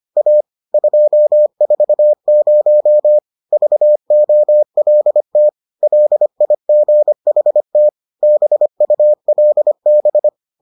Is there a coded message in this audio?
Yes: Morse code from 0.2 to 10.3 s, heard as a keyed tone, about 25 wpm, 600 Hz. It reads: A 240 VOLT LIGHT BULB